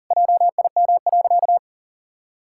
Morse code, heard as a keyed tone, about 37 wpm, 700 Hz. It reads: JIM.
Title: Pangram Challenge v2 - 40wpm